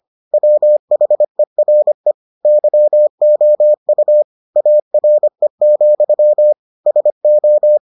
Morse code, heard as a keyed tone, about 25 wpm, 600 Hz. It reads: WHERE YOU ARE, SO